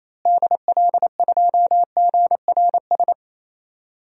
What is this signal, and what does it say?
Morse code, heard as a keyed tone, about 27 wpm, 700 Hz. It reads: DL2GRS
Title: DXCC-100 Call Signs - 6 Characters - Repeated 3x - 28wpm